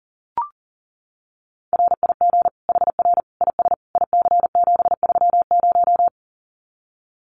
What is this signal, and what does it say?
Morse code, heard as a keyed tone, about 40 wpm, 700 Hz. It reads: RIG HR IS IC730
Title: Common QSO Phrases 40wpm